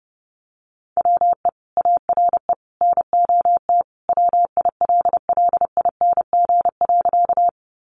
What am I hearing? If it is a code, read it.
Morse code, heard as a keyed tone, about 30 wpm, 700 Hz. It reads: WE ARE NOT WILLING.
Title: Sentences from Top 1000 Words 30wpm